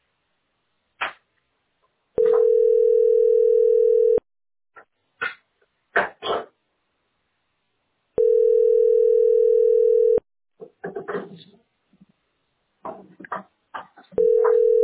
Ringback tone (North America), 440 + 480 Hz.